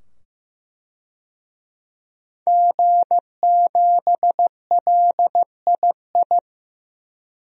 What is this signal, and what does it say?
Morse code, heard as a keyed tone, about 15 wpm, 700 Hz. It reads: G7LII
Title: DXCC-100 Call Signs - 5 Characters - 15wpm